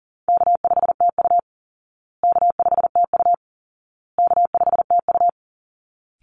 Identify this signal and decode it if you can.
Morse code, heard as a keyed tone, about 40 wpm, 700 Hz. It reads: K5TU K5TU K5TU
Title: Call Signs Repeated 2x - 40wpm